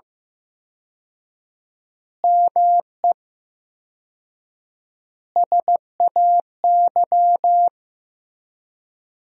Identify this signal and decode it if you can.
Morse code, heard as a keyed tone, about 15 wpm, 700 Hz. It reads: ME SAY
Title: 4x Word-Spacing - Sentences from Top 500 Words - 15wpm